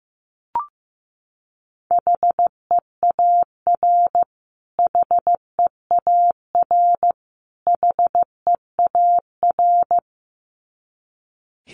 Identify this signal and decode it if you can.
Morse code, heard as a keyed tone, about 14 wpm, 700 Hz. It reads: HEAR HEAR HEAR
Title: Top 500 Words Repeated 15wpm